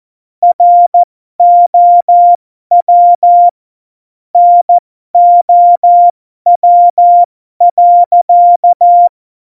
Morse code, heard as a keyed tone, about 14 wpm, spaced as for 12 wpm, 700 Hz. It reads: ROW NOW.
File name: Morsekurs / Witze